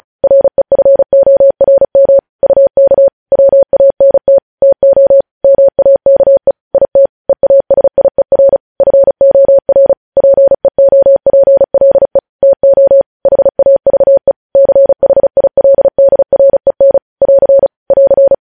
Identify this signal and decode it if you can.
Morse code, heard as a keyed tone, about 36 wpm, 550 Hz. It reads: REFORM UK WANT TO MAKE IT EASIER FOR PEOPLE TO HAVE CHILDREN <AR> <AR>
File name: podcast / News headlines-1x-35x35